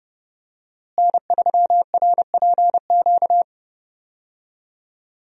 Morse code, heard as a keyed tone, about 30 wpm, 700 Hz. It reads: N3RPQ
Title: USA Call Signs - 5 Characters - Repeated 3x - 30wpm